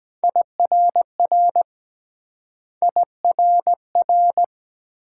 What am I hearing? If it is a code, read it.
Morse code, heard as a keyed tone, about 22 wpm, 700 Hz. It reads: IRR IRR